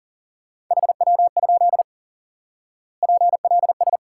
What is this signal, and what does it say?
Morse code, heard as a keyed tone, about 40 wpm, 700 Hz. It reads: HW? PLS
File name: IQE Course 3 - Lesson 89 - YAGI - Two Elements - 40wpm